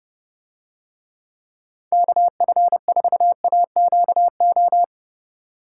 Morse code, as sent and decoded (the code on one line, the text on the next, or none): -.- ..-. ....- .- --.- ---
KF4AQO